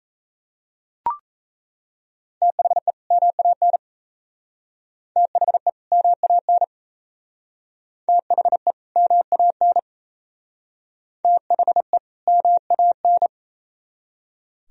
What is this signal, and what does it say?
Morse code, heard as a keyed tone, about 34 wpm, 700 Hz. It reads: THE MAN THE MAN THE MAN THE MAN
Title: Speed-Racing - Sets of 2 Words 28wpm